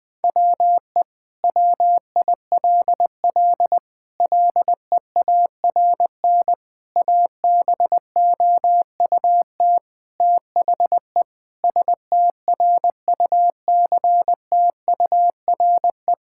Morse code, heard as a keyed tone, about 20 wpm, 700 Hz. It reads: WE WILL LEARN ABOUT THE STRUCTURE
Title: Sentences from Top 1000 Words - Review Part 6 - 20wpm